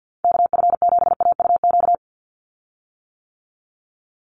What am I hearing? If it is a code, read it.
Morse code, heard as a keyed tone, about 50 wpm, 700 Hz. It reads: KF7AUQ
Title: USA Call Signs 50wpm